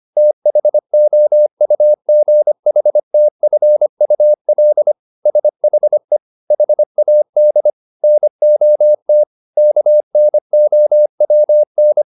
Morse code, heard as a keyed tone, about 25 wpm, 600 Hz. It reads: THOUGHTFUL SHE HAD NOT KNOWN